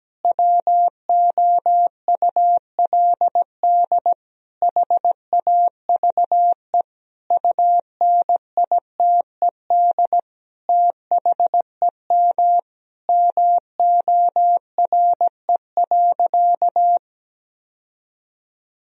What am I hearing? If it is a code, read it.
Morse code, heard as a keyed tone, about 17 wpm, 700 Hz. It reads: WOULD HAVE UNITED THEM MORE.
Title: Sentences from Top 1000 Words - Review Part 4 - 17wpm